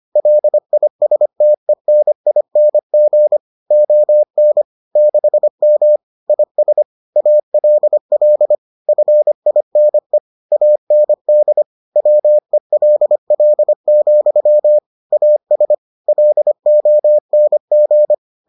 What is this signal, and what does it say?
Morse code, heard as a keyed tone, about 25 wpm, 600 Hz. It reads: LISTENING ON 6M IS ALL FINE AND WELL, AS LONG